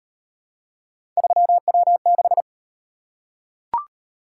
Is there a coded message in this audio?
Yes: Morse code from 1.2 to 2.4 s, heard as a keyed tone, about 38 wpm, 700 Hz. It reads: 3W6